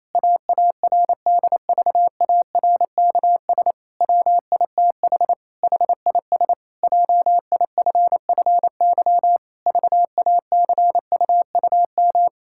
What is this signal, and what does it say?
Morse code, heard as a keyed tone, about 27 wpm, 700 Hz. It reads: AARDVARKS WITH HIS JIFFY VACUUM
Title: Pangram Challenge v2 - 28wpm